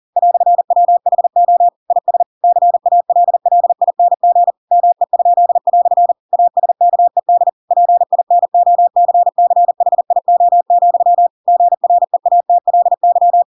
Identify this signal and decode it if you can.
Morse code, heard as a keyed tone, about 40 wpm, 700 Hz. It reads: "WHO IS CALLING ME?" ASKED PINOCCHIO, GREATLY
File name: Pinocchio / Pinocchio ch4 40 40 ik1hge